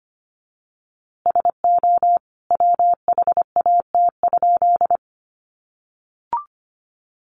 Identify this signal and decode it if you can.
Morse code, heard as a keyed tone, about 25 wpm, 700 Hz. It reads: SO WHAT?
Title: A Sound Of Thunder 25wpm